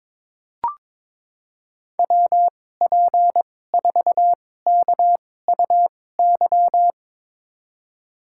Morse code, heard as a keyed tone, about 21 wpm, spaced as for 16 wpm, 700 Hz. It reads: WP4KUY
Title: USA Call Signs - 6 Characters - 2x Character Spacing - 22wpm